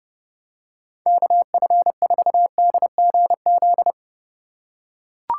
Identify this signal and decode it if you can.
Morse code, heard as a keyed tone, about 29 wpm, 700 Hz. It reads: KF4DGZ